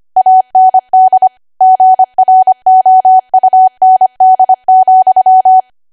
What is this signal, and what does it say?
Morse code, heard as a keyed tone, about 25 wpm, 750 Hz. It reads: AND GROUND,
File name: cwa / W1AW-Level3 / 25WPM